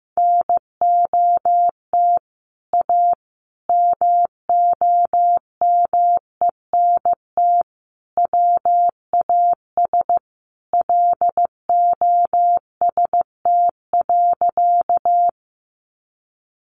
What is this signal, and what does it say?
Morse code, heard as a keyed tone, about 15 wpm, 700 Hz. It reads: NOT A MOMENT WAS LOST.
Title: Sentences from Top 1400 Words 15wpm